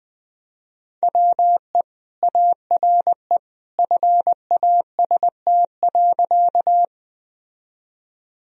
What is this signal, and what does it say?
Morse code, heard as a keyed tone, about 21 wpm, 700 Hz. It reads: WE ARE FAST.
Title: Sentences from Top 1000 Words - Review Part 2 - 20wpm